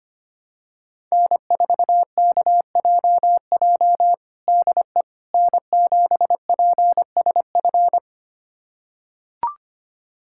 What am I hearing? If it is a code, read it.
Morse code, heard as a keyed tone, about 25 wpm, 700 Hz. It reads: N4KJJ DE N7PSF
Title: USA Callsign DE Callsign - 5 Characters - 25wpm